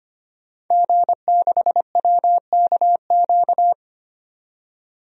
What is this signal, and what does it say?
Morse code, heard as a keyed tone, about 25 wpm, 700 Hz. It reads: G6WKQ
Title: DXCC-100 Call Signs - 5 Characters - 25wpm